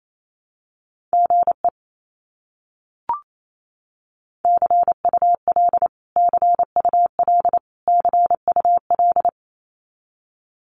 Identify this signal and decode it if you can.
Morse code, heard as a keyed tone, about 28 wpm, 700 Hz. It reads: GE CUL CUL CUL
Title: Abbreviations Repeated 3x - 28wpm